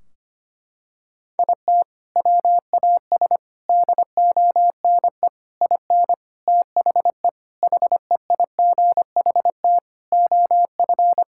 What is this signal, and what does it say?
Morse code, heard as a keyed tone, about 25 wpm, 700 Hz. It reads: IT WAS DONE IN THE HEIGHT OF
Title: Sentences from Top 1600 Words 25wpm